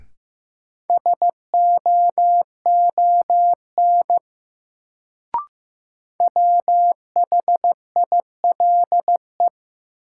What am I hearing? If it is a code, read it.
Morse code, heard as a keyed tone, about 15 wpm, 700 Hz. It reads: SOON WHILE
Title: Sets of 3 Words - New Word Drill - 15wpm